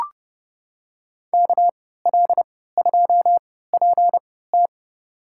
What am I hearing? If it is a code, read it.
Morse code, heard as a keyed tone, about 30 wpm, spaced as for 17 wpm, 700 Hz. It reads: KL2PT
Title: USA Call Signs - 5 Characters - 3x Character Spacing - 30wpm